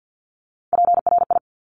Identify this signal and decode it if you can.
Morse code, heard as a keyed tone, about 50 wpm, 700 Hz. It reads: PRI
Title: Top Three Letter Combinations in Words 50wpm